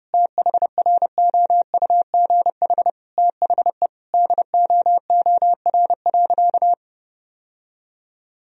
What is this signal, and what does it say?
Morse code, heard as a keyed tone, about 30 wpm, 700 Hz. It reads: THROUGH THE DOOR.
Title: Sentences from Top 1000 Words - Review Part 1 - 30wpm